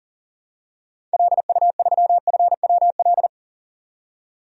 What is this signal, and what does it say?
Morse code, heard as a keyed tone, about 40 wpm, 700 Hz. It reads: LU3FWL
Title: DXCC-100 Call Signs - 6 Characters - 40wpm